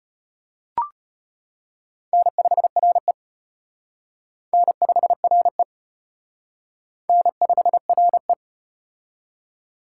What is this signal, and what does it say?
Morse code, heard as a keyed tone, about 33 wpm, 700 Hz. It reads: N5RE N5RE N5RE